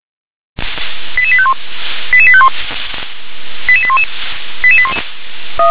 Five-tone selective call (ZVEI), heard as quick runs of five tones, tones of about 70 ms. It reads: 88851 88851 88811 88811
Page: SDR/ZVEI